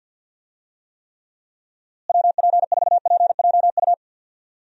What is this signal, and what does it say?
Morse code, heard as a keyed tone, about 52 wpm, 700 Hz. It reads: WP4PJU
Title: USA Call Signs - 6 Characters - Repeated 2x - 50wpm